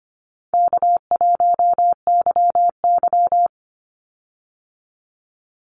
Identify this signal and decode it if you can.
Morse code, heard as a keyed tone, about 25 wpm, 700 Hz. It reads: K1YY
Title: Easy Call Signs Repeated 3x - 25wpm